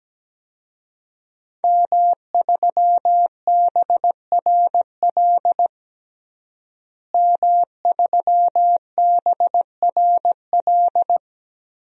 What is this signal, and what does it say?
Morse code, heard as a keyed tone, about 17 wpm, 700 Hz. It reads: M3BRL M3BRL
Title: DXCC-100 Call Signs - 5 Characters - Repeated 3x - 17wpm